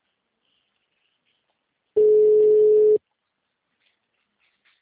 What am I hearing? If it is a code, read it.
Ringback tone (Europe, 425 Hz).